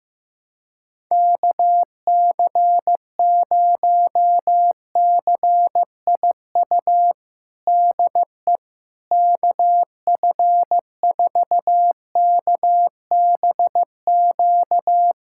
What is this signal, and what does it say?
Morse code, heard as a keyed tone, about 15 wpm, 700 Hz. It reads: KC0CIU DE KF4KBQ
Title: USA Callsign DE Callsign - 6 Characters - 15wpm